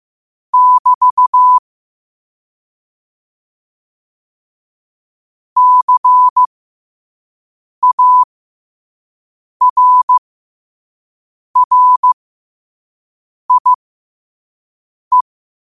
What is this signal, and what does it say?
Morse code, heard as a keyed tone, about 15 wpm, spaced as for 5 wpm, 1000 Hz. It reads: <BT> CARRIE
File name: qotdfiles / 5wpm